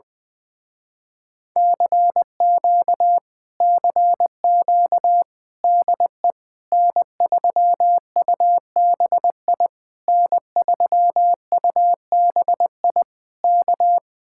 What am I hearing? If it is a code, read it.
Morse code, heard as a keyed tone, about 19 wpm, 700 Hz. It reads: CQ CQ DE N3UBI N3UBI K
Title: Speed-Racing Common QSO Phrases - 20wpm